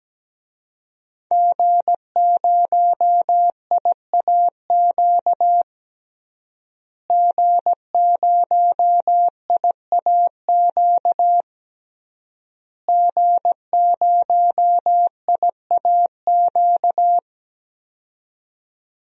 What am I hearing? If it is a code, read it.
Morse code, heard as a keyed tone, about 17 wpm, 700 Hz. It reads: G0IAQ G0IAQ G0IAQ
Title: DXCC-100 Call Signs - 5 Characters - Repeated 3x - 17wpm